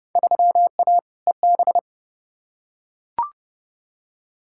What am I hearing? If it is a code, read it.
Morse code, heard as a keyed tone, about 30 wpm, 700 Hz. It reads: 3A EB
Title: Field Day Exchanges Encore - 30wpm